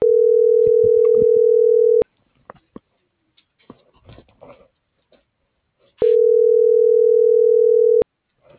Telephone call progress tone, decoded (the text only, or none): ringback tone